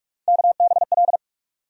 Morse code, heard as a keyed tone, about 42 wpm, 700 Hz. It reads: KBL
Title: USA Call Sign Postfixes - 3 Letters 45wpm